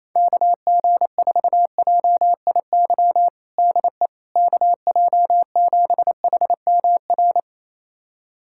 Morse code, heard as a keyed tone, about 28 wpm, 700 Hz. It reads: KG4JIY DE KJ7HMR
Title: USA Callsign DE Callsign 28wpm